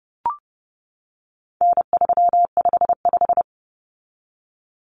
Morse code, heard as a keyed tone, about 30 wpm, 700 Hz. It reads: N355
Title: Common Cut Numbers - 4-Digit 30wpm